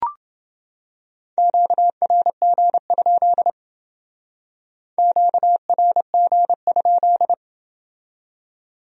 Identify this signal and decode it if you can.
Morse code, heard as a keyed tone, about 28 wpm, 700 Hz. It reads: QRG? QRG?